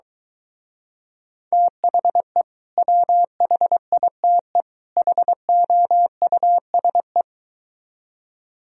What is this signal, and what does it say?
Morse code, heard as a keyed tone, about 23 wpm, 700 Hz. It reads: THE WHITE HOUSE